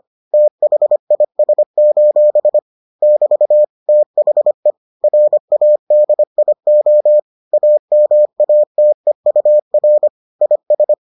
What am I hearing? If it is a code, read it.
Morse code, heard as a keyed tone, about 25 wpm, 600 Hz. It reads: THIS: <BT> THE RADIO AMATEUR IS